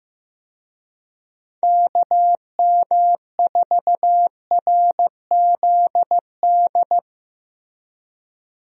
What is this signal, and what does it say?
Morse code, heard as a keyed tone, about 16 wpm, 700 Hz. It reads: KM4RZD
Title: USA Call Signs - 6 Characters - Repeated 2x - 15wpm